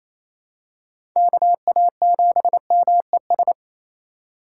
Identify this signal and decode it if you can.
Morse code, heard as a keyed tone, about 28 wpm, 700 Hz. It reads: KA7MES
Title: USA Call Signs - 8x Character Spacing - 28wpm